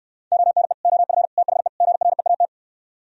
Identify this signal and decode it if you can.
Morse code, heard as a keyed tone, about 67 wpm, 700 Hz. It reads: ONE OF THE GREAT